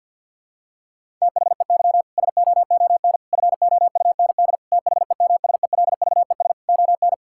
Morse code, heard as a keyed tone, about 51 wpm, 700 Hz. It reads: THEY SOON FOUND THEMSELVES ON
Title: Sentences from Top 1000 Words - Review Part 5 - 50wpm